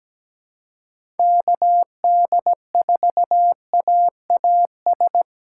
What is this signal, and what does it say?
Morse code, heard as a keyed tone, about 17 wpm, 700 Hz. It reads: KD4AAS
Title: USA Call Signs - 6 Characters - 4x Character Spacing - 17wpm